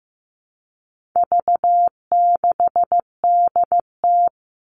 Morse code, heard as a keyed tone, about 15 wpm, 700 Hz. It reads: V6DT
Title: All Cut Numbers - 4-Digit 15wpm